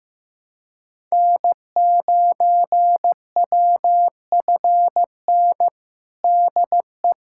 Morse code, heard as a keyed tone, about 15 wpm, 700 Hz. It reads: N9WFN DE
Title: USA Callsign DE Callsign - 5 Characters - 15wpm